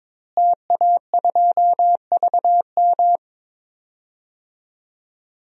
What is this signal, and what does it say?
Morse code, heard as a keyed tone, about 22 wpm, 700 Hz. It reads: TA2VM
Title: DXCC-100 Call Signs - 5 Characters - Repeated 3x - 22wpm